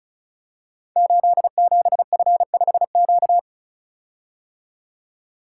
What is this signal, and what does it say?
Morse code, heard as a keyed tone, about 36 wpm, 700 Hz. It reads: 87F5Q
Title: Code Groups - 5 Letter:Numbers - 35wpm